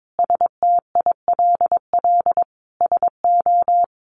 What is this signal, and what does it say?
Morse code, heard as a keyed tone, about 22 wpm, 700 Hz. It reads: STILL SO
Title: Sentences from Top 400 Words 22wpm